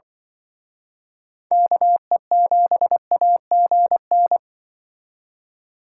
Morse code, heard as a keyed tone, about 25 wpm, 700 Hz. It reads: KE7AGN
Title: Speed Racing - USA 2x3 Call Signs 20wpm